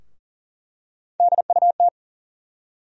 Morse code, heard as a keyed tone, about 40 wpm, 700 Hz. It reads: DUT
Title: USA Call Sign Postfixes - 3 Letters 40wpm